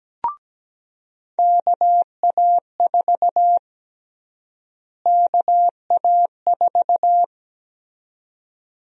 Morse code, heard as a keyed tone, about 17 wpm, 700 Hz. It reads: KA4 KA4
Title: USA Call Sign Prefixes - Repeated 2x - 17wpm